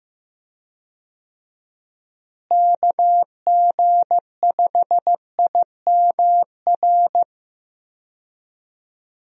Morse code, heard as a keyed tone, about 15 wpm, 700 Hz. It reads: KG5IMR
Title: USA Call Signs - 6 Characters - Repeated 2x - 15wpm